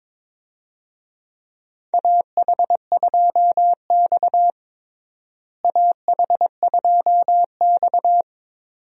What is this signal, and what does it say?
Morse code, heard as a keyed tone, about 22 wpm, 700 Hz. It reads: AH2X AH2X
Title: USA Call Signs - 4 Characters - Repeated 2x - 22wpm